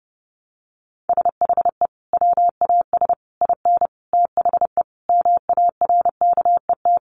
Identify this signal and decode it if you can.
Morse code, heard as a keyed tone, about 30 wpm, 700 Hz. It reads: SHE WAS IN THE MARKET
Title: Sentences from Top 600 Words 30wpm